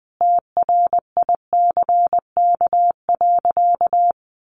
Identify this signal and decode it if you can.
Morse code, heard as a keyed tone, about 20 wpm, 700 Hz. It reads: TRICK.
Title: Sentences from Top 1500 Words 20wpm